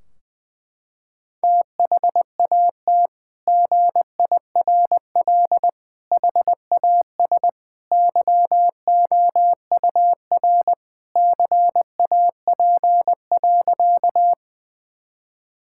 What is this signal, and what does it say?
Morse code, heard as a keyed tone, about 20 wpm, 700 Hz. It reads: THAT GIRL HAS YOUR CAP.
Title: Sentences from Top 1600 Words 20wpm